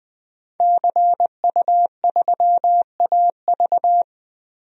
Morse code, heard as a keyed tone, about 20 wpm, 700 Hz. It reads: CU3AV
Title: DXCC-100 Call Signs - 5 Characters - Repeated 2x - 20wpm